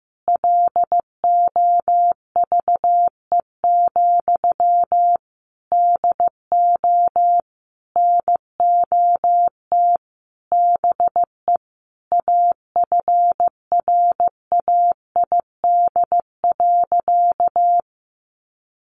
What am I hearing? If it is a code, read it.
Morse code, heard as a keyed tone, about 15 wpm, 700 Hz. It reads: LOVE, DO NOT BE AFRAID.
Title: Sentences from Top 1500 Words 15wpm